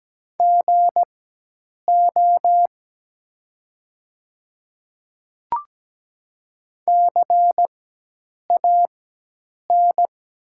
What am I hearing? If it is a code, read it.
Morse code, heard as a keyed tone, about 17 wpm, spaced as for 8 wpm, 700 Hz. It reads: GO CAN
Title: 4x Character Spacing - Top 100 Words - 17wpm